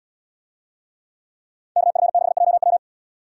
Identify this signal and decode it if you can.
Morse code, heard as a keyed tone, about 68 wpm, 700 Hz. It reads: KF6JW